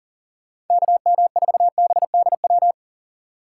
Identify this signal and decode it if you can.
Morse code, heard as a keyed tone, about 49 wpm, 700 Hz. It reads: KM4BDW
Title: Speed Racing - USA Call Signs 40wpm